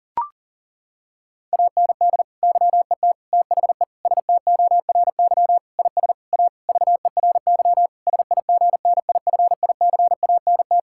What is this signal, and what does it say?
Morse code, heard as a keyed tone, about 40 wpm, 690 Hz. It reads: AND YET THE STORY IS A VERY SIGNIFICANT
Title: Sentences from Top 1000 Words - Review Part 6 - 40wpm